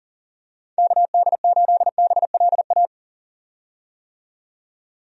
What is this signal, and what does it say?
Morse code, heard as a keyed tone, about 40 wpm, 700 Hz. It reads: KD8BLA
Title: USA Call Signs - 6 Characters - 40wpm